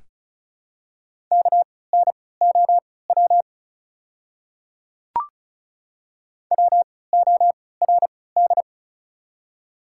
Morse code, heard as a keyed tone, about 35 wpm, spaced as for 20 wpm, 690 Hz. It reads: KNOW WORD
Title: 3x Character Spacing - Top 100 Words - 35wpm